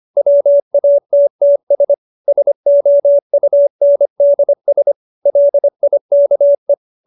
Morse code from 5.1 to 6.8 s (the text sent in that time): LIKE